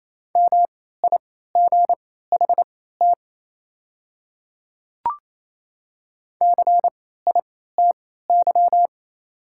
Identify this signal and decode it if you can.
Morse code, heard as a keyed tone, about 28 wpm, spaced as for 16 wpm, 700 Hz. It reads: MIGHT CITY
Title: 3x Character Spacing - Top 300 Words - 28wpm